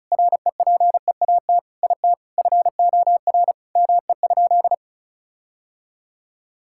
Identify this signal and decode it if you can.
Morse code, heard as a keyed tone, about 36 wpm, 700 Hz. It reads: REPEAT IT FOR ME?
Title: Sentences from Top 1600 Words 35wpm